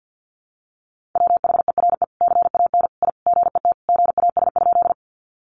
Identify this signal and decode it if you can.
Morse code, heard as a keyed tone, about 50 wpm, 700 Hz. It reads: WHERE CAN I GET GAS?
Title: Sentences from Top 1100 Words 50wpm